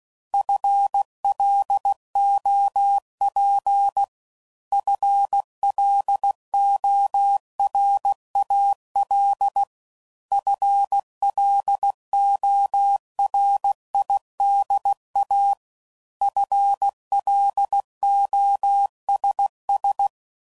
Morse code, heard as a keyed tone, about 16 wpm, 800 Hz. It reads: FLOP FLORAL FLORIDA FLOSS